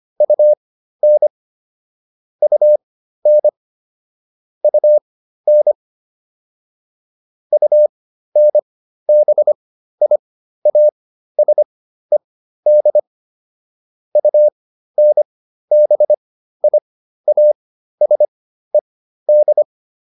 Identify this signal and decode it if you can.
Morse code, heard as a keyed tone, about 25 wpm, spaced as for 13 wpm, 600 Hz. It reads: UN UN UN UNBIASED UNBIASED